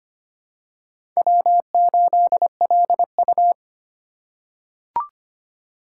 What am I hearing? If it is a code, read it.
Morse code, heard as a keyed tone, about 25 wpm, 700 Hz. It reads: W8LU
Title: USA Call Signs - 4 Characters - 2x Character Spacing - 25wpm